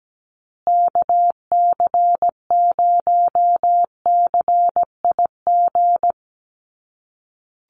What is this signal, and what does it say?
Morse code, heard as a keyed tone, about 17 wpm, 700 Hz. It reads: KC0CIG
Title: USA Call Signs Repeated 2x 17wpm